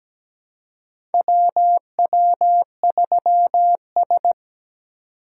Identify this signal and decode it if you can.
Morse code, heard as a keyed tone, about 17 wpm, 700 Hz. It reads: WW3S
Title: USA Call Signs - 4 Characters - 4x Character Spacing - 17wpm